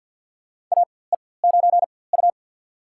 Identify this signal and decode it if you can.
Morse code, heard as a keyed tone, about 50 wpm, spaced as for 23 wpm, 690 Hz. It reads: AE9U
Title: USA Call Signs - 4x Character Spacing - 50wpm